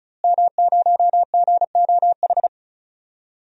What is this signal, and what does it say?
Morse code, heard as a keyed tone, about 34 wpm, 700 Hz. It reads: M0GOH
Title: DXCC-100 Call Signs - 5 Characters - Repeated 2x - 35wpm